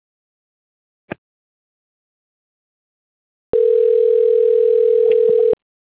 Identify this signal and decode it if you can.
Ringback tone (North America), 440 + 480 Hz.